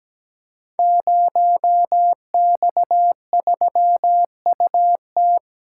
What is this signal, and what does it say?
Morse code, heard as a keyed tone, about 17 wpm, 700 Hz. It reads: 0X3UT
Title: Code Groups - 5 Letter:Numbers - 17wpm